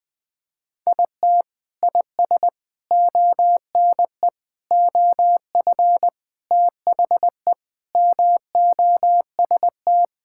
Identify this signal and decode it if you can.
Morse code, heard as a keyed tone, about 20 wpm, 700 Hz. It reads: IT IS ONE OF THE MOST
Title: Sentences with 3-Word Phrases - Spoken Beginning - 20wpm